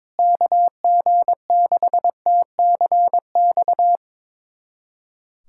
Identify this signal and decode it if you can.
Morse code, heard as a keyed tone, about 22 wpm, 700 Hz. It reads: KG6TCX
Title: USA Callsign DE Callsign 22wpm